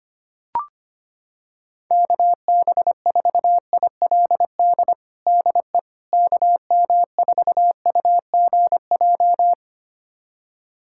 Morse code, heard as a keyed tone, about 25 wpm, 700 Hz. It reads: KB4ILD DE KM4UGJ
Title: USA Callsign DE Callsign - 6 Characters - 25wpm